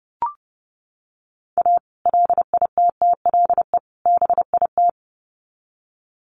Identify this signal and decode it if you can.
Morse code, heard as a keyed tone, about 30 wpm, 700 Hz. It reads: A LITTLE BIT
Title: Sets of 3 Words 30wpm